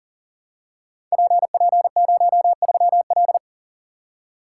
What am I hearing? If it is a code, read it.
Morse code, heard as a keyed tone, about 40 wpm, 700 Hz. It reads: PP03L
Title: Code Groups - 5 Letter:Numbers - 40wpm